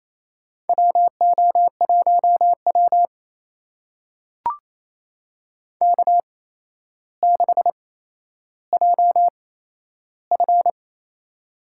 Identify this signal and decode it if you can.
Morse code, heard as a keyed tone, about 28 wpm, spaced as for 12 wpm, 700 Hz. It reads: WO1W K6JF
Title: USA Call Signs - 4 Characters - 8x Character Spacing - 28wpm